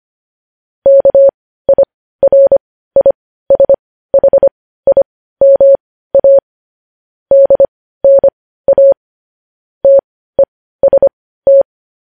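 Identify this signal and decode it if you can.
Morse code, heard as a keyed tone, about 25 wpm, spaced as for 15 wpm, 550 Hz. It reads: KIRISHIMA DNA TEST
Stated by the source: Morse news headlines sent at 15 wpm.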